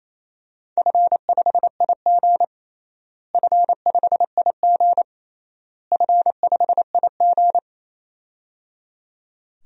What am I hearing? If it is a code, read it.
Morse code, heard as a keyed tone, about 27 wpm, 700 Hz. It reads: F5IG F5IG F5IG
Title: DXCC-100 Call Signs - 4 Characters - Repeated 3x - 28wpm